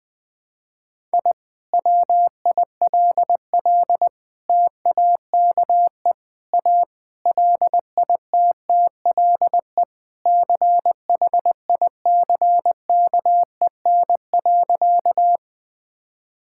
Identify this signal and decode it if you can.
Morse code, heard as a keyed tone, about 20 wpm, 700 Hz. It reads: I WILL TAKE A LITTLE CHICKEN.
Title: Sentences from Top 1000 Words - Review Part 5 - 20wpm